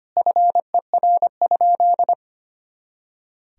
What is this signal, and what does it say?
Morse code, heard as a keyed tone, about 25 wpm, 700 Hz. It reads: FER?
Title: Sentences from Top 200 Words and Abbreviations 25wpm